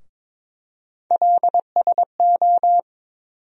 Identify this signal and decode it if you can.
Morse code, heard as a keyed tone, about 21 wpm, 700 Hz. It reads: LSO
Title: USA Call Sign Postfixes - 3 Letters 22wpm